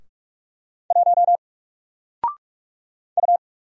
Morse code, heard as a keyed tone, about 44 wpm, 700 Hz. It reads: 1 U
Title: Single Letter-Number 45wpm